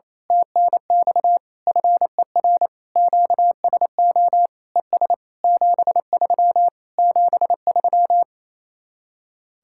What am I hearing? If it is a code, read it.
Morse code, heard as a keyed tone, about 27 wpm, 700 Hz. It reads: TNX FER QSO ES 73 73
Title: Speed-Racing Common QSO Phrases Encore - 28wpm